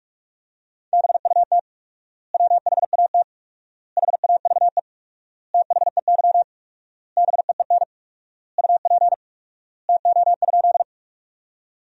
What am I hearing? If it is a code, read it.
Morse code, heard as a keyed tone, about 46 wpm, 700 Hz. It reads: BUT WHAT HAVE THEY BEEN UP TO?
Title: 4x Word-Spacing - Sentences from Top 100 Words - 45wpm